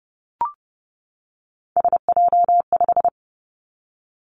Morse code, heard as a keyed tone, about 30 wpm, 700 Hz. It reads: SJ5
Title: Three Challenging Letter-Number Combinations 30wpm